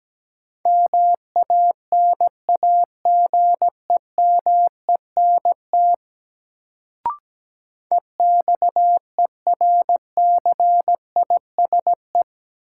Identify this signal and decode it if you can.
Morse code, heard as a keyed tone, about 17 wpm, 700 Hz. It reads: MANAGEMENT EXERCISE
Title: Sentences from Top 1000 Words - Review Part 4 - New Word Drill - 17wpm